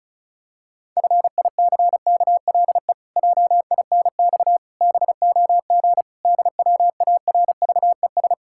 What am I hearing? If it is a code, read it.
Morse code, heard as a keyed tone, about 36 wpm, 700 Hz. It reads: FICKLE JINX BOG DWARVES
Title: Pangram Challenge v2 - 35wpm